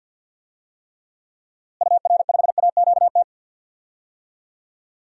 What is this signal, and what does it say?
Morse code, heard as a keyed tone, about 53 wpm, 700 Hz. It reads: UR5AQT
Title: DXCC-100 Call Signs - 6 Characters - Repeated 2x - 50wpm